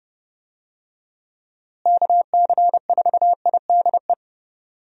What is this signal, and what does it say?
Morse code, heard as a keyed tone, about 30 wpm, 700 Hz. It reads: KC4IDE
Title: USA Call Signs - 6 Characters - Repeated 2x - 30wpm